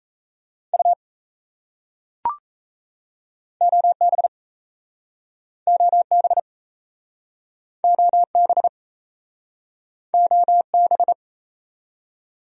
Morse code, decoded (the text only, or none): U OB OB OB OB